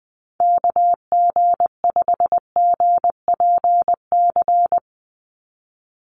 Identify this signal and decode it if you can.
Morse code, heard as a keyed tone, about 20 wpm, 700 Hz. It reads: KG5GPC
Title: USA Call Signs 20wpm